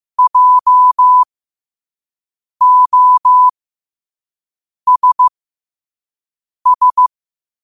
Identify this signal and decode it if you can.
Morse code, heard as a keyed tone, about 15 wpm, spaced as for 5 wpm, 1000 Hz. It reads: JOSS
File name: qotdfiles / 5wpm